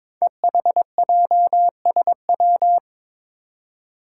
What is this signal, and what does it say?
Morse code, heard as a keyed tone, about 22 wpm, 700 Hz. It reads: EHJSW
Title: Code Groups - 5 Letters - 22wpm